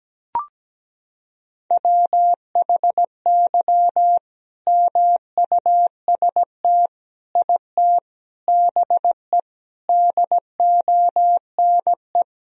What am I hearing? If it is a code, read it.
Morse code, heard as a keyed tone, about 17 wpm, 700 Hz. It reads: WHY MUST IT BE DONE
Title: Sentences from Top 1000 Words - Review Part 3 - 17wpm